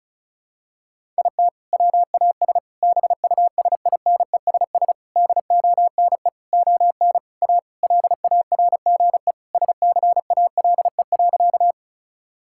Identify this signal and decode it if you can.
Morse code, heard as a keyed tone, about 36 wpm, 700 Hz. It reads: IT WAS BUSINESS DONE ON A LARGE SCALE.
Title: Sentences from Top 1000 Words - Review Part 5 - 35wpm